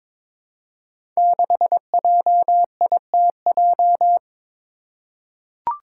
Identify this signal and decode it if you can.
Morse code, heard as a keyed tone, about 21 wpm, 700 Hz. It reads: 6JITJ
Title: Five Challenging Letter-Number Combinations 22wpm